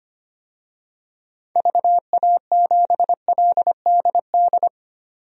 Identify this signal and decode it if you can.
Morse code, heard as a keyed tone, about 25 wpm, 690 Hz. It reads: VA7LDD
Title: DXCC-100 Call Signs - 6 Characters - 25wpm